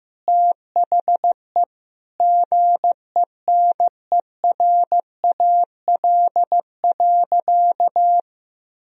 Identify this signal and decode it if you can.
Morse code, heard as a keyed tone, about 15 wpm, 700 Hz. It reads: THE GENERAL.
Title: Sentences from Top 1000 Words - Review Part 3 - 15wpm